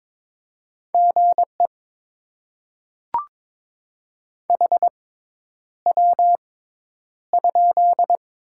Morse code, heard as a keyed tone, about 22 wpm, spaced as for 10 wpm, 690 Hz. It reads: GE HW?
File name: Top 20 QSO Elements - 6x Character Spacing - 22wpm